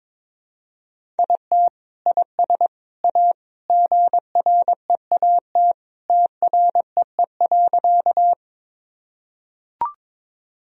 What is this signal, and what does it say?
Morse code, heard as a keyed tone, about 22 wpm, 700 Hz. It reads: IT IS A GREAT TREE.
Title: Sentences from Top 1000 Words - Review Part 1 - 22wpm